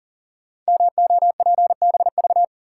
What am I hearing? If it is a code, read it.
Morse code, heard as a keyed tone, about 42 wpm, 700 Hz. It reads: MOPBV
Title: Code Groups - 5 Letters - 40wpm